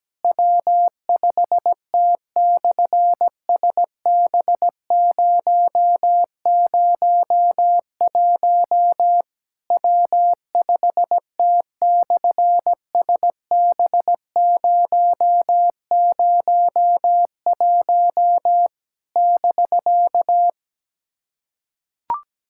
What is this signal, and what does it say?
Morse code, heard as a keyed tone, about 17 wpm, 700 Hz. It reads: W5T/SB001 W5T/SB001 <BK>
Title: SOTA QSO Practice 17wpm